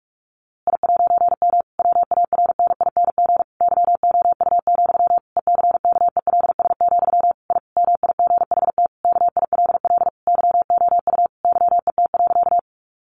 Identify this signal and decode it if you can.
Morse code, heard as a keyed tone, about 44 wpm, 700 Hz. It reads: I'M WARNING YOU, ECKELS, I MIGHT KILL YOU YET.
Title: A Sound Of Thunder 45wpm